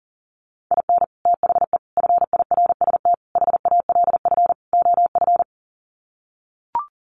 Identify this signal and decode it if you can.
Morse code, heard as a keyed tone, about 40 wpm, 700 Hz. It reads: IN THE FIRST HALF OF
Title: Sets of 5 Words 40wpm